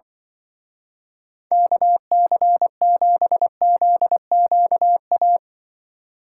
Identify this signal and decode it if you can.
Morse code, heard as a keyed tone, about 25 wpm, 700 Hz. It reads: KC7ZQA
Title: Speed Racing - USA 2x3 Call Signs 20wpm